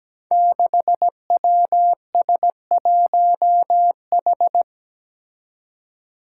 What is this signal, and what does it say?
Morse code, heard as a keyed tone, about 17 wpm, 700 Hz. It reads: 6WS1H
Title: Five Challenging Letter-Number Combinations 17wpm